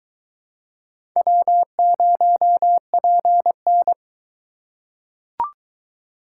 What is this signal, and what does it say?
Morse code, heard as a keyed tone, about 22 wpm, 700 Hz. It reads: W0PN